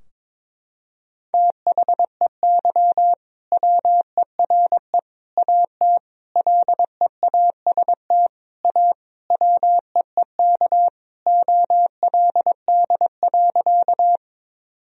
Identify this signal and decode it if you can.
Morse code, heard as a keyed tone, about 22 wpm, 700 Hz. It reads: THEY WERE AT LEAST A WEEK OLD.
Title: Sentences from Top 1000 Words - Review Part 2 - 22wpm